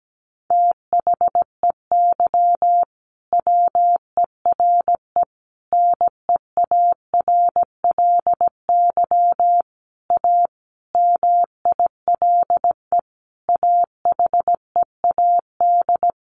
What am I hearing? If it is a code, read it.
Morse code, heard as a keyed tone, about 17 wpm, 700 Hz. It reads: THEY WERE NEARLY A MILE AHEAD
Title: Sentences from Top 1300 Words 17wpm